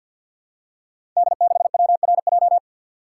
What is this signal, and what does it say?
Morse code, heard as a keyed tone, about 52 wpm, 700 Hz. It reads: D6WRJ
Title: Code Groups - 5 Letter:Numbers - 50wpm